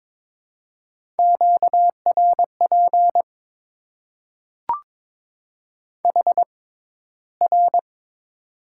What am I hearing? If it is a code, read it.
Morse code, heard as a keyed tone, about 22 wpm, spaced as for 14 wpm, 700 Hz. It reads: QRP HR
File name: Top 90 QSO Elements - 6x Character Spacing - 22wpm